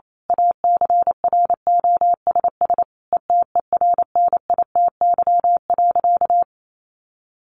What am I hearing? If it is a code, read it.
Morse code, heard as a keyed tone, about 28 wpm, 700 Hz. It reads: ACROSS ETERNITY.
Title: A Sound Of Thunder 28wpm